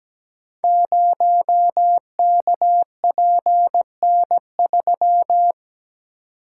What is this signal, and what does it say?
Morse code, heard as a keyed tone, about 17 wpm, 700 Hz. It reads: 0KPN3
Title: Code Groups - 5 Letter:Numbers - 17wpm